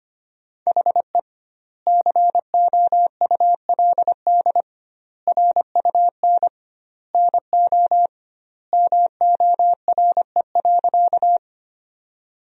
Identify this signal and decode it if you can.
Morse code, heard as a keyed tone, about 25 wpm, 700 Hz. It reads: HE COULD RUN NO MORE.
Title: 2x Word-Spacing - Sentences from Top 300 Words - 25wpm